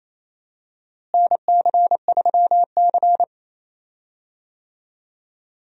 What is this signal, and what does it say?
Morse code, heard as a keyed tone, about 28 wpm, 700 Hz. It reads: NC3C
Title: USA Call Signs - 4 Characters - Repeated 3x - 28wpm